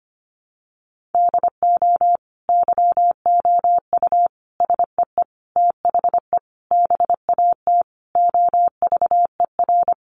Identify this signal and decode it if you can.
Morse code, heard as a keyed tone, about 25 wpm, 700 Hz. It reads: DO YOU SEE THE BAT OVER
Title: Sentences from Top 1500 Words 25wpm